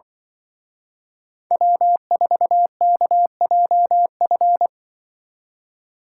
Morse code, heard as a keyed tone, about 24 wpm, 700 Hz. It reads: W4KJF
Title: Speed Racing - USA Call Signs 20wpm